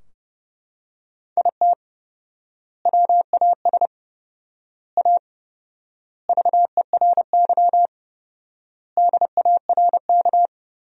Morse code, heard as a keyed tone, about 30 wpm, 700 Hz. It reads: IT WAS A VERY DARK
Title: 4x Word-Spacing - Sentences from Top 500 Words - 30wpm